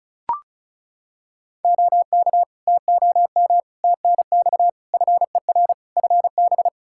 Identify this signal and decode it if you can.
Morse code, heard as a keyed tone, about 34 wpm, 700 Hz. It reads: OK TOM TNX FER FB
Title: Practice QSO - Exchange 4 - Ending - 35wpm